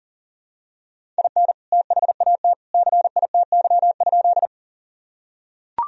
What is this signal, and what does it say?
Morse code, heard as a keyed tone, about 40 wpm, 700 Hz. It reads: IN THAT CITY?
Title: Sentences from Top 1000 Words - Review Part 1 - 40wpm